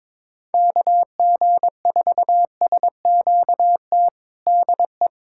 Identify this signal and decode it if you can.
Morse code, heard as a keyed tone, about 22 wpm, 700 Hz. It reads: KG4SQT DE
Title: USA Callsign DE Callsign - 6 Characters - 22wpm